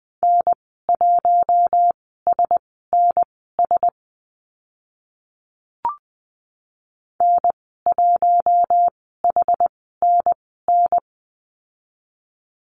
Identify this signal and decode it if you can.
Morse code, heard as a keyed tone, about 20 wpm, spaced as for 14 wpm, 700 Hz. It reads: N1SNS N1HNN
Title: 062 Lesson 11g - Letter h - Call Signs 20wpm